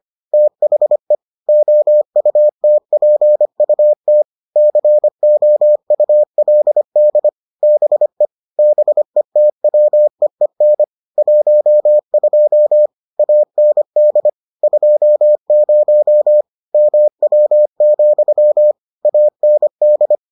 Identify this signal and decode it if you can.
Morse code, heard as a keyed tone, about 25 wpm, 600 Hz. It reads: THE OUTPUT COULD BE BETWEEN 12 AND 20 MW, AND